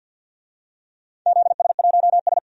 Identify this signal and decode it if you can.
Morse code, heard as a keyed tone, about 52 wpm, 700 Hz. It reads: ZS1S
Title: DXCC-100 Call Signs - 4 Characters - Repeated 2x - 50wpm